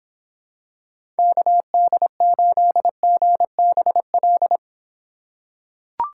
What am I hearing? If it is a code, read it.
Morse code, heard as a keyed tone, about 25 wpm, 700 Hz. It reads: KD8GBL